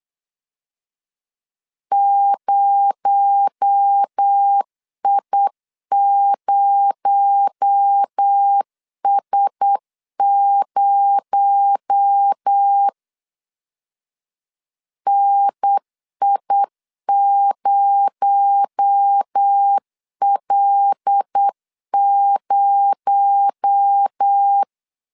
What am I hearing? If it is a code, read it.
Morse code, heard as a keyed tone, about 8 wpm, 800 Hz. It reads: 0I0S0 NI0L0
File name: LEZIONI / 6 PKX0 / 6 PKX0 8wpm 3341